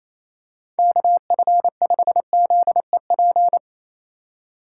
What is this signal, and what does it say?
Morse code, heard as a keyed tone, about 28 wpm, 700 Hz. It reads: KF5ZEP
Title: USA Call Signs - 6 Characters - 4x Character Spacing - 28wpm